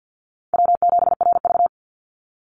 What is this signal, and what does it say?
Morse code, heard as a keyed tone, about 50 wpm, 700 Hz. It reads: W7RV
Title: NA SOTA Chaser Callsigns 50wpm